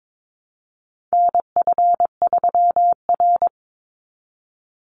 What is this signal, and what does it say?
Morse code, heard as a keyed tone, about 22 wpm, 700 Hz. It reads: NF3R
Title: Call Signs Repeated 2x - 22wpm